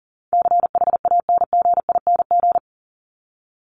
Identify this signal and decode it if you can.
Morse code, heard as a keyed tone, about 40 wpm, 700 Hz. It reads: CHANGING
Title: ING Suffix - Top 100 Words with Base Words 40wpm